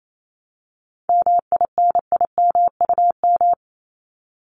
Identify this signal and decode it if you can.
Morse code, heard as a keyed tone, about 28 wpm, 700 Hz. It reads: MINIMUM
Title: Top 1251-1500 Words 28wpm